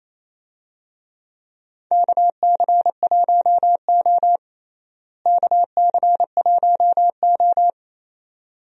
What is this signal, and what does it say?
Morse code, heard as a keyed tone, about 28 wpm, 700 Hz. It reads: KC1O KC1O
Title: USA Call Signs - 4 Characters - Repeated 2x - 28wpm